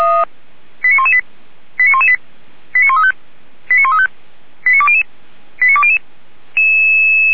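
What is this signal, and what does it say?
Five-tone selective call (ZVEI), heard as quick runs of five tones, tones of about 70 ms. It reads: 89118 89118 89135 89135 89200 89200
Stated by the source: ZVEI five-tone selective call 89103 received via SDR.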